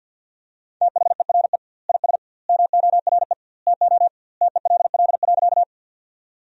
Morse code, heard as a keyed tone, about 52 wpm, 700 Hz. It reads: THERE IS MORE TO TELL.
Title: 2x Word-Spacing - Sentences from Top 200 Words - 50wpm